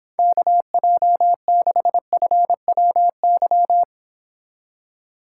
Morse code, heard as a keyed tone, about 25 wpm, 700 Hz. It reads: KJ6FWY